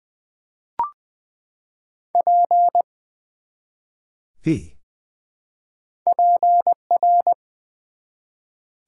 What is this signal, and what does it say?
Morse code, heard as a keyed tone, about 20 wpm, 690 Hz. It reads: P PR
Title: Build A Word - 1 Letter At A Time - 4-Letter Words - 20wpm